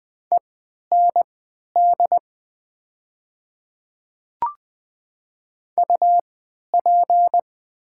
Morse code, heard as a keyed tone, about 20 wpm, spaced as for 11 wpm, 700 Hz. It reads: END UP
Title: 3x Character Spacing - Top 200 Words - 20wpm